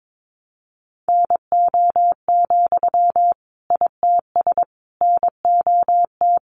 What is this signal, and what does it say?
Morse code, heard as a keyed tone, about 22 wpm, 700 Hz. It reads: NO, ITS NOT
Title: Sentences from Top 600 Words 22wpm